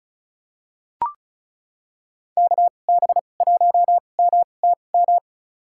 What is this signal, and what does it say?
Morse code, heard as a keyed tone, about 34 wpm, spaced as for 25 wpm, 700 Hz. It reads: KB1MTM
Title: USA Call Signs - 6 Characters - 2x Character Spacing - 35wpm